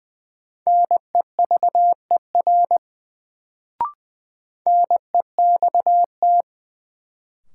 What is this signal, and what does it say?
Morse code, heard as a keyed tone, about 20 wpm, 700 Hz. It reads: NEVER NEXT
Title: Sentences from Top 300 Words - New Word Drill - 20wpm